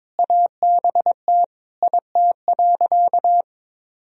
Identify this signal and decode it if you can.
Morse code, heard as a keyed tone, about 22 wpm, 700 Hz. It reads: ABT IT.
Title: Sentences from Top 200 Words and Abbreviations 22wpm